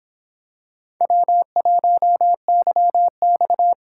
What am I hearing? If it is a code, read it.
Morse code, heard as a keyed tone, about 26 wpm, 700 Hz. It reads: W1YX